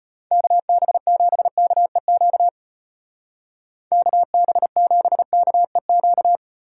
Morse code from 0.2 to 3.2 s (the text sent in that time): KB7KEQ